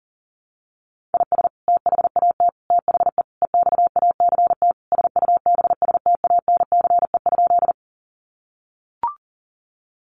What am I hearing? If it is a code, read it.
Morse code, heard as a keyed tone, about 40 wpm, 700 Hz. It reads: IS THAT THE EXACT SUBSTANCE?
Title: Sentences from Top 1500 Words 40wpm